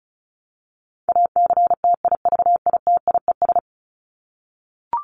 Morse code, heard as a keyed tone, about 34 wpm, 700 Hz. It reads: ACTIVITIES
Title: ES:S Suffix - Top 100 Words with Base Words 35wpm